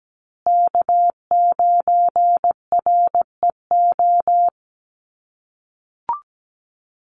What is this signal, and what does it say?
Morse code, heard as a keyed tone, about 17 wpm, 700 Hz. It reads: K9REO
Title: USA Call Signs Repeated 3x 17wpm